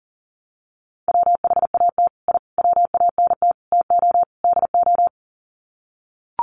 Morse code, heard as a keyed tone, about 40 wpm, 700 Hz. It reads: WHAT I WANT TO DO
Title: Sets of 5 Words Encore 40wpm